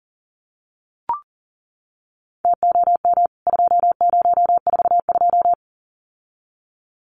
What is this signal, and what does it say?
Morse code, heard as a keyed tone, about 37 wpm, 700 Hz. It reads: TOM 2042
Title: CWT Exchange 40wpm